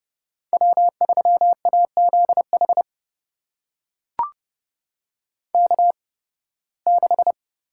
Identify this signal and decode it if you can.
Morse code, heard as a keyed tone, about 30 wpm, spaced as for 20 wpm, 700 Hz. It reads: W3AZH K6
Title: USA Call Signs - 5 Characters - 8x Character Spacing - 30wpm